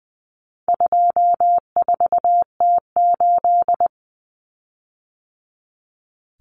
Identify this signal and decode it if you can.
Morse code, heard as a keyed tone, about 20 wpm, 700 Hz. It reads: 24T8
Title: Common Cut Numbers - 4-Digit 20wpm